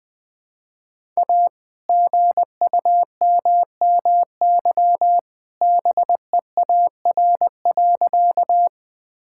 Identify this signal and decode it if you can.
Morse code, heard as a keyed tone, about 20 wpm, 700 Hz. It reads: A GUMMY BEAR.
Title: All-Age Appropriate Jokes - 20wpm